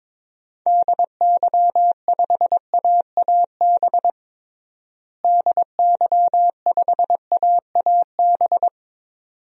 Morse code, heard as a keyed tone, about 22 wpm, 700 Hz. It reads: DY5AAB DY5AAB
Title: DXCC-100 Call Signs - 6 Characters - Repeated 2x - 22wpm